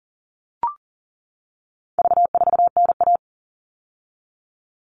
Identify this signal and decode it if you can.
Morse code, heard as a keyed tone, about 40 wpm, 700 Hz. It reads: V4NA
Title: All Cut Numbers - 4-Digit 40wpm